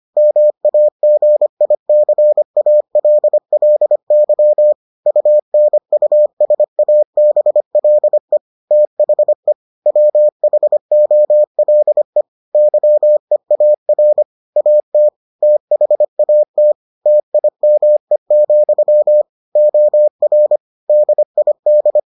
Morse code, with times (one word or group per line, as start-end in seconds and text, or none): 0.2-4.7 MAGICALLY
5.1-8.4 UNUSABLE
8.7-9.5 THE
9.9-12.2 WHOLE
12.5-14.2 YEAR
14.6-15.1 AT
15.4-16.7 THAT
17.1-19.2 TIME,
19.6-20.6 OR
20.9-22.0 DID